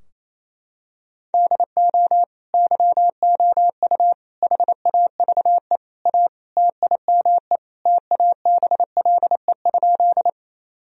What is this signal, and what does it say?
Morse code, heard as a keyed tone, about 28 wpm, 700 Hz. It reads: DO YOU HAVE A TIME TABLE?
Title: Sentences from Top 1000 Words - Review Part 2 - 28wpm